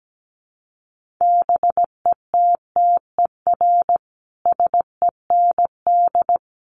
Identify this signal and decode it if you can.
Morse code, heard as a keyed tone, about 17 wpm, 700 Hz. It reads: BETTER SEND
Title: Sentences from Top 1500 Words 17wpm